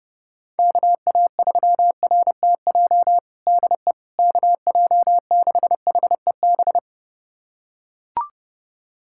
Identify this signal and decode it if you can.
Morse code, heard as a keyed tone, about 30 wpm, 700 Hz. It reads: KA3RTJ DE KJ6HEB
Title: USA Callsign DE Callsign 30wpm